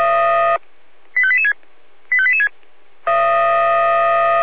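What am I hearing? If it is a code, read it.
Five-tone selective call (ZVEI), heard as quick runs of five tones, tones of about 70 ms. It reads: 85886 85886
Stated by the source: ZVEI five-tone selective call 85241, received by SDR.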